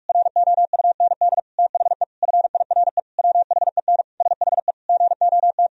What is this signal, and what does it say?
Morse code, heard as a keyed tone, about 42 wpm, 700 Hz. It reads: ROUND THE FIRE WHEN SHE GOT
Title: Sentences from Top 1000 Words - Review Part 1 - 45wpm